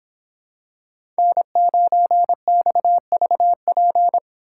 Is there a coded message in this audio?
Yes: Morse code from 1.2 to 4.2 s, heard as a keyed tone, about 26 wpm, 700 Hz. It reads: N9XVP